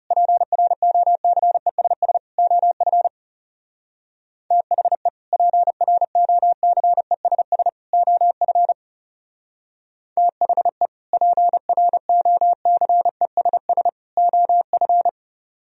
Morse code, heard as a keyed tone, about 39 wpm, 700 Hz. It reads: PROCESS OF THE PROCESS OF THE PROCESS OF